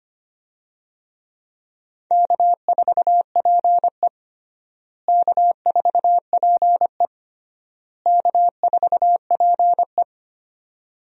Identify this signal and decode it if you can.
Morse code, heard as a keyed tone, about 25 wpm, 700 Hz. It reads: K4PE K4PE K4PE
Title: USA Call Signs - 4 Characters - Repeated 3x - 25wpm